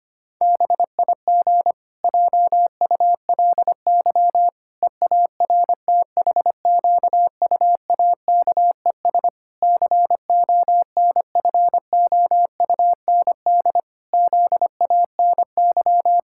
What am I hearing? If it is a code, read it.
Morse code, heard as a keyed tone, about 25 wpm, 700 Hz. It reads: BIG JULY EARTHQUAKES CONFOUND ZANY
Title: Pangram Challenge v2 - 25wpm